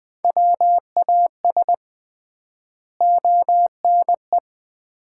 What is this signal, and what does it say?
Morse code, heard as a keyed tone, about 20 wpm, 700 Hz. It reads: WAS ONE
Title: 3x Word-Spacing - Sets of 5 Words - 20wpm